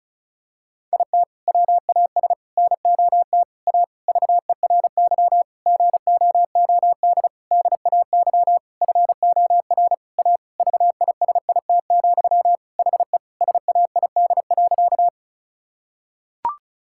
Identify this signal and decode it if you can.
Morse code, heard as a keyed tone, about 36 wpm, 700 Hz. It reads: IT WAS NOT A VERY GOOD DAY FOR A VISIT, HE SAID.
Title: Sentences from Top 1000 Words - Review Part 6 - 35wpm